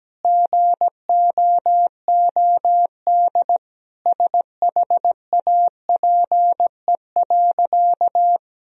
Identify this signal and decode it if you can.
Morse code, heard as a keyed tone, about 17 wpm, 700 Hz. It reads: GOOD SHAPE.
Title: Sentences from Top 1000 Words - Review Part 3 - 17wpm